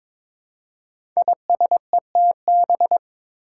Morse code, heard as a keyed tone, about 22 wpm, 700 Hz. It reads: ISETB
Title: Five Challenging Letter-Number Combinations 22wpm